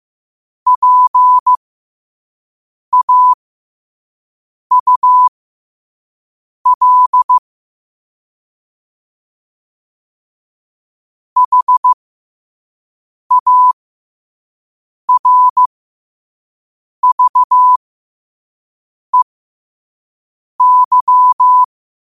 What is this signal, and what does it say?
Morse code, heard as a keyed tone, about 15 wpm, spaced as for 5 wpm, 1000 Hz. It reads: PAUL HARVEY
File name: 5wpm